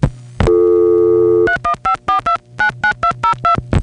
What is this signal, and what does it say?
DTMF, heard as telephone touch tones, tones of about 110 ms. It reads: 3154299202